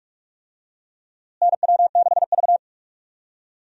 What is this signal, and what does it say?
Morse code, heard as a keyed tone, about 47 wpm, 700 Hz. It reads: NW6V
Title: USA Call Signs - 4 Characters - Repeated 3x - 45wpm